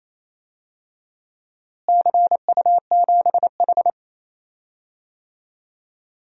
Morse code, heard as a keyed tone, about 28 wpm, 700 Hz. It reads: CU7H
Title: DXCC-100 Call Signs - 4 Characters - Repeated 3x - 28wpm